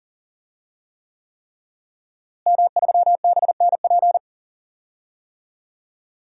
Morse code, heard as a keyed tone, about 40 wpm, 700 Hz. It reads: M3BNP
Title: DXCC-100 Call Signs - 5 Characters - Repeated 3x - 40wpm